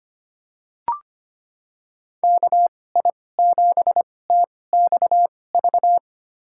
Morse code, heard as a keyed tone, about 25 wpm, spaced as for 18 wpm, 700 Hz. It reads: KI7TXV
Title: USA Call Signs - 6 Characters - 2x Character Spacing - 25wpm